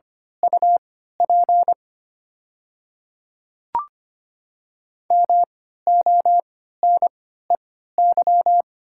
Morse code, heard as a keyed tone, about 25 wpm, spaced as for 14 wpm, 700 Hz. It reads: UP MONEY
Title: 3x Character Spacing - Top 500 Words - 25wpm